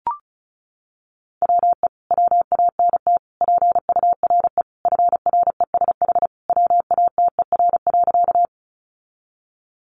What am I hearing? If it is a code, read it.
Morse code, heard as a keyed tone, about 34 wpm, 700 Hz. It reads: WE WANT PURE FRESH WATER.
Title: Sentences from Top 1500 Words 35wpm